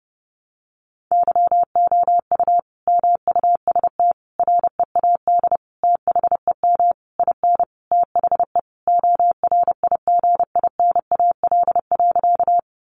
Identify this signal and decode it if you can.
Morse code, heard as a keyed tone, about 30 wpm, 700 Hz. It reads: YOU MUST READ THEM IN THE ORIGINAL.
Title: Sentences from Top 900 Words 30wpm